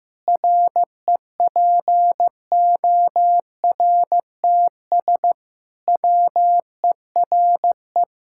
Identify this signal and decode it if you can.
Morse code, heard as a keyed tone, about 15 wpm, 700 Hz. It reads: REPORTS WERE
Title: Pangram Challenge v2 - 15wpm